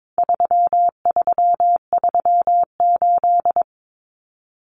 Morse code, heard as a keyed tone, about 22 wpm, 700 Hz. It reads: 3338
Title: Common Cut Numbers - 4-Digit 22wpm